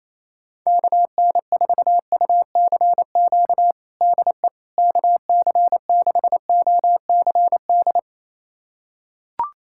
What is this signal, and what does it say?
Morse code, heard as a keyed tone, about 28 wpm, 700 Hz. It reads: KN4UCQ DE KC6OCD
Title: USA Callsign DE Callsign - 6 Characters - 28wpm